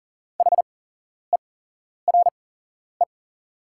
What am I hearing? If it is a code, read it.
Morse code, heard as a keyed tone, about 40 wpm, spaced as for 11 wpm, 690 Hz. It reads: HERE
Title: 8x Character Spacing - Top 200 Words - 40wpm